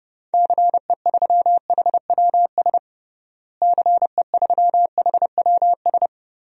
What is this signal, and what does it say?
Morse code, heard as a keyed tone, about 30 wpm, 700 Hz. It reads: CE3HWS CE3HWS
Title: DXCC-100 Call Signs - 6 Characters - Repeated 3x - 30wpm